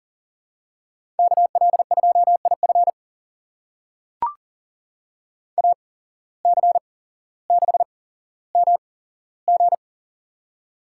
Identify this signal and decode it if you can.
Morse code, heard as a keyed tone, about 40 wpm, spaced as for 17 wpm, 700 Hz. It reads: KL2IF AC6MG
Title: USA Call Signs - 5 Characters - 8x Character Spacing - 40wpm